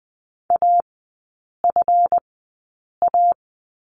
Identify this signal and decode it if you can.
Morse code, heard as a keyed tone, about 20 wpm, spaced as for 8 wpm, 700 Hz. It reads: AFA
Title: 138 Lesson 23e - Letter f - Three Characters 20wpm